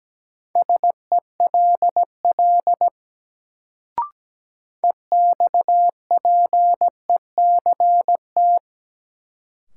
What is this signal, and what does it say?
Morse code, heard as a keyed tone, about 17 wpm, 700 Hz. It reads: SELL EXPECT
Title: Sentences from Top 1100 Words - New Word Drill Alt - 17wpm